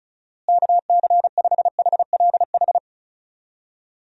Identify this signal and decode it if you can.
Morse code, heard as a keyed tone, about 36 wpm, 700 Hz. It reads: KC5HLH
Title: USA Call Signs - 6 Characters - Repeated 2x - 35wpm